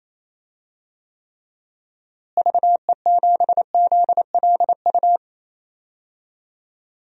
Morse code, heard as a keyed tone, about 28 wpm, 700 Hz. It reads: VE7ZLU
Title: DXCC-100 Call Signs - 6 Characters - Repeated 3x - 28wpm